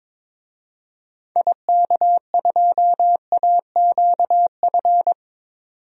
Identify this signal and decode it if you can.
Morse code, heard as a keyed tone, about 22 wpm, 700 Hz. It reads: IK2AQF
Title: DXCC-100 Call Signs - 6 Characters - 22wpm